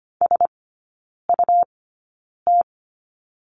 Morse code, heard as a keyed tone, about 25 wpm, spaced as for 9 wpm, 700 Hz. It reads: SUT
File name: Lesson 17e - Letter u - Three Characters 25wpm